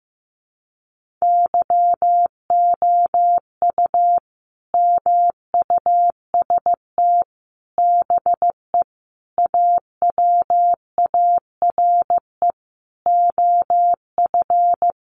Morse code, heard as a keyed tone, about 15 wpm, 700 Hz. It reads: YOU MUST BE AWARE OF
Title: Sentences from Top 1200 Words 15wpm